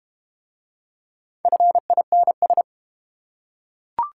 Morse code, heard as a keyed tone, about 31 wpm, 700 Hz. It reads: FINS